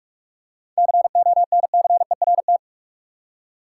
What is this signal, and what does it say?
Morse code, heard as a keyed tone, about 46 wpm, 700 Hz. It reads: CONCERT
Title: Seven Letter Words - 45wpm